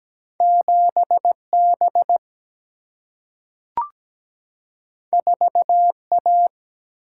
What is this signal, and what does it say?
Morse code, heard as a keyed tone, about 17 wpm, 700 Hz. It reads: 7 B 4 A
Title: Single-Letter Single-Number 17wpm